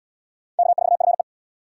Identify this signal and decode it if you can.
Morse code, heard as a keyed tone, about 67 wpm, 700 Hz. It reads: K4FE